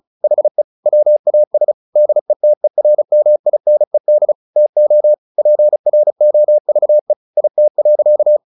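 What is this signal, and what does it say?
Morse code, heard as a keyed tone, about 34 wpm, 600 Hz. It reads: HE WAS DETERMINED TO PROVE IT.
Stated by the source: Morse practice at 35 wpm from CW med Gnister, sending Ken Follett's novel Never in English.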